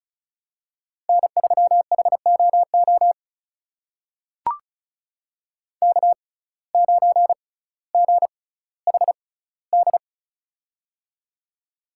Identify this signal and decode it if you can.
Morse code, heard as a keyed tone, about 36 wpm, spaced as for 18 wpm, 700 Hz. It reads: N3HOO K9GHD
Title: USA Call Signs - 5 Characters - 6x Character Spacing - 35wpm